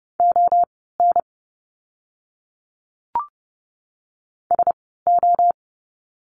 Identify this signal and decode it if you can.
Morse code, heard as a keyed tone, about 30 wpm, spaced as for 17 wpm, 700 Hz. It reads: ON SO
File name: course / 30:10wpm / 230 Lesson 37f - Letter x - Words 30wpm